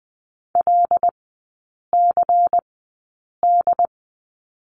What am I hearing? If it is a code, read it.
Morse code, heard as a keyed tone, about 20 wpm, spaced as for 8 wpm, 700 Hz. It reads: LCD
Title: 093 Lesson 16e - Letter c - Three Characters 20wpm